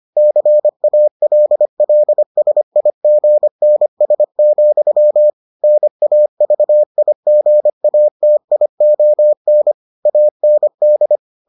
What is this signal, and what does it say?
Morse code, heard as a keyed tone, about 25 wpm, 600 Hz. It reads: CALLSIGNS, NAVIGATION AND